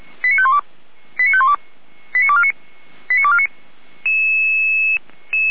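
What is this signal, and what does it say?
Five-tone selective call (ZVEI), heard as quick runs of five tones, tones of about 70 ms. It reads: 89512 89512 89249 89249
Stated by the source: ZVEI five-tone call received on SDR, from the feuerwehr-orlamuende.org fire brigade audio archive.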